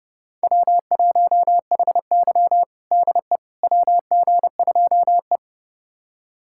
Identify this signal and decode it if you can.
Morse code, heard as a keyed tone, about 32 wpm, 700 Hz. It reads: W1HY DE WG2E
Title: USA Callsign DE Callsign - 4 Characters - 30wpm